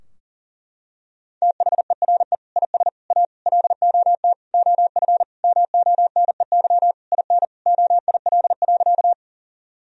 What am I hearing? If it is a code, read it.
Morse code, heard as a keyed tone, about 40 wpm, 700 Hz. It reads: THERE IS A LOT OF MONEY IN OIL.
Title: Sentences from Top 1000 Words - Review Part 4 - 40wpm